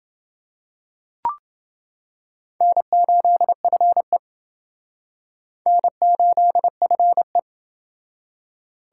Morse code, heard as a keyed tone, about 28 wpm, 700 Hz. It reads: N8FE N8FE